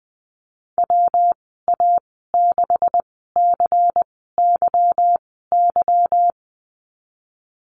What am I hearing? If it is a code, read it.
Morse code, heard as a keyed tone, about 19 wpm, spaced as for 14 wpm, 700 Hz. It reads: WA6CYY
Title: 147 Lesson 24g - Letter y - Call Signs 20wpm